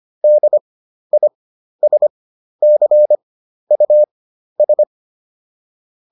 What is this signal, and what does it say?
Morse code, heard as a keyed tone, about 25 wpm, spaced as for 12 wpm, 600 Hz. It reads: DISCUS